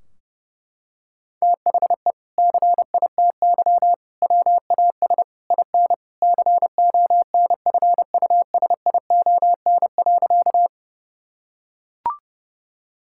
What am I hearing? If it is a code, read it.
Morse code, heard as a keyed tone, about 30 wpm, 700 Hz. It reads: THE CITY WAS IN CONFUSION.
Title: Sentences from Top 1600 Words 30wpm